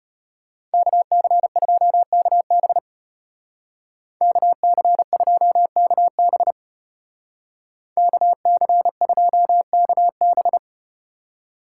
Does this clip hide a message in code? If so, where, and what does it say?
Yes: Morse code from 0.7 to 10.6 s, heard as a keyed tone, about 33 wpm, 700 Hz. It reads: KC2KB KC2KB KC2KB